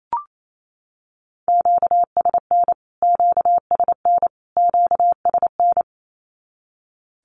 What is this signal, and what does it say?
Morse code, heard as a keyed tone, about 28 wpm, 700 Hz. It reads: QSN QSN QSN
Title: Q-Codes Repeated 3x 28wpm